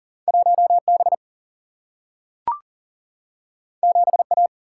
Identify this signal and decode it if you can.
Morse code, heard as a keyed tone, about 39 wpm, 700 Hz. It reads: 1B 7A
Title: Single-Letter Single-Number 40wpm